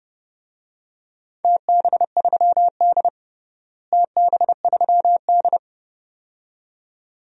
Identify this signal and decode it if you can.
Morse code, heard as a keyed tone, about 31 wpm, 700 Hz. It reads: TB3D TB3D
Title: DXCC-100 Call Signs - 4 Characters - Repeated 2x - 30wpm